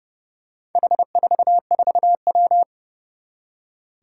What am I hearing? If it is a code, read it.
Morse code, heard as a keyed tone, about 30 wpm, 700 Hz. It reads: H44W
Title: Four Challenging Letter-Number Combinations 30wpm